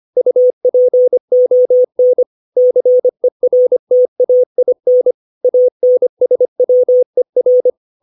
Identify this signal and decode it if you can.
Morse code, heard as a keyed tone, about 25 wpm, 500 Hz. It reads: UPON CERTAIN ANSWER
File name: 632 words / 632 Words 25WPM